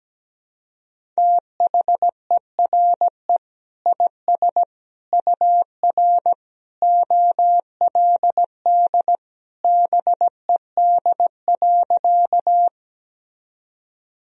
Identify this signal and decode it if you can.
Morse code, heard as a keyed tone, about 17 wpm, 700 Hz. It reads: THERE IS UR OLD BED.
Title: Sentences from Top 500 Words and Abbreviations 17wpm